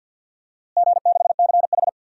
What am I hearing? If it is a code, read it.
Morse code, heard as a keyed tone, about 46 wpm, 700 Hz. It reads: G6CH
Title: DXCC-100 Call Signs - 4 Characters - Repeated 2x - 50wpm